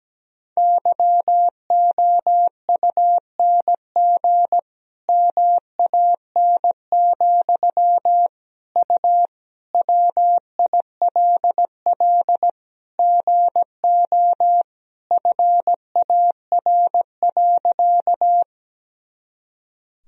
Morse code, heard as a keyed tone, about 17 wpm, 700 Hz. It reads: YOUNG MAN, U WILL GO FAR.
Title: Sentences from Top 400 Words and Abbreviations 17wpm